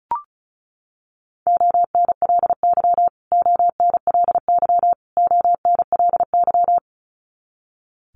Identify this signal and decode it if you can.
Morse code, heard as a keyed tone, about 34 wpm, 700 Hz. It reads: ONLY ONLY ONLY
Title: Top 500 Words Repeated 35wpm